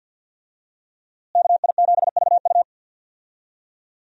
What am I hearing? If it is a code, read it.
Morse code, heard as a keyed tone, about 47 wpm, 700 Hz. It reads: KI7VU
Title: USA Call Signs - 5 Characters - Repeated 2x - 50wpm